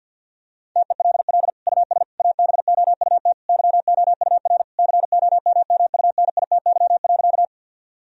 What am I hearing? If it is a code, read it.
Morse code, heard as a keyed tone, about 48 wpm, 700 Hz. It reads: TELL US ABOUT YOUR COMMUNITY.
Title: Sentences from Top 1000 Words - Review Part 4 - 50wpm